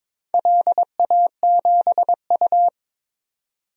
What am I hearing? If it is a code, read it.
Morse code, heard as a keyed tone, about 22 wpm, 700 Hz. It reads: LA7U
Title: DXCC-100 Call Signs - 4 Characters - 22wpm